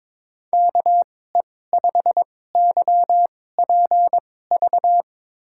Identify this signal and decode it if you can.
Morse code, heard as a keyed tone, about 22 wpm, spaced as for 16 wpm, 700 Hz. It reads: KE5YPV
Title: USA Call Signs - 6 Characters - 2x Character Spacing - 22wpm